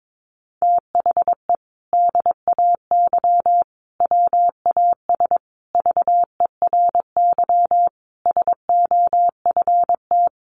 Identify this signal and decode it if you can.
Morse code, heard as a keyed tone, about 22 wpm, 700 Hz. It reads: THE DAY WAS VERY SOFT
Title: Sentences from Top 1500 Words 22wpm